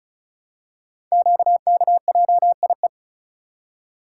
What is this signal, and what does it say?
Morse code, heard as a keyed tone, about 36 wpm, 690 Hz. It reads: QKJIE
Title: Code Groups - 5 Letters - 35wpm